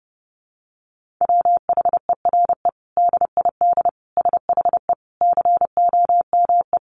Morse code, heard as a keyed tone, about 30 wpm, 700 Hz. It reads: WHERE DID SHE COME
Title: Sentences from Top 200 Words 30wpm